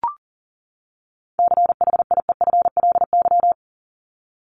Morse code, heard as a keyed tone, about 40 wpm, 700 Hz. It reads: CHIEFLY
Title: LY Suffix - Top 100 Words with Base Words 40wpm